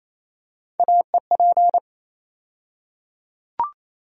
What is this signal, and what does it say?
Morse code, heard as a keyed tone, about 28 wpm, 700 Hz. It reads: AEP
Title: USA Call Sign Postfixes - 3 Letters 28wpm